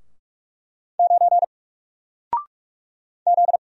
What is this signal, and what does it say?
Morse code, heard as a keyed tone, about 44 wpm, 700 Hz. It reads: 9 Z
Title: Single Letter-Number Alt 45wpm